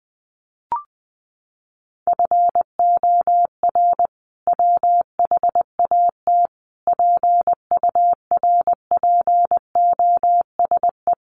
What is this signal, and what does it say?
Morse code, heard as a keyed tone, about 20 wpm, 700 Hz. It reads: FOR WHAT PURPOSE
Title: Sentences from Top 800 Words 20wpm